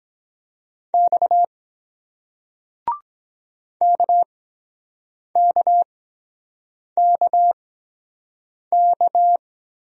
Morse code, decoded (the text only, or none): X KKKK